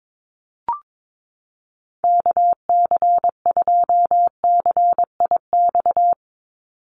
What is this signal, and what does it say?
Morse code, heard as a keyed tone, about 22 wpm, 700 Hz. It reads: KC2CIX
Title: USA Call Signs 22wpm